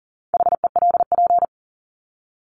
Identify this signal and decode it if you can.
Morse code, heard as a keyed tone, about 40 wpm, 700 Hz. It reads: HELP
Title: Top 200 Words 40wpm